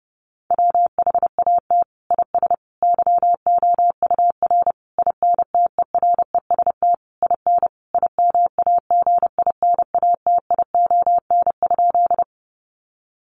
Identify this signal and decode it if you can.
Morse code, heard as a keyed tone, about 30 wpm, 700 Hz. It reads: WHAT IS YOUR INTEREST IN IMAGINATION?
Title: Sentences from Top 1300 Words 30wpm